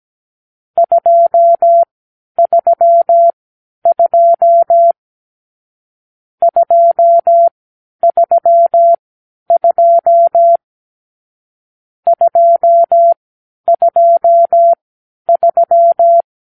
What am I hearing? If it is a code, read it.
Morse code, heard as a keyed tone, about 17 wpm, spaced as for 10 wpm, 680 Hz. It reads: 232 232 223
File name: k7qo / Lesson 44 (1-3)